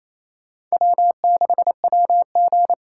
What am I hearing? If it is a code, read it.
Morse code, heard as a keyed tone, about 34 wpm, 700 Hz. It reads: W6WG
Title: Speed Racing - USA 1x2 Call Signs 28wpm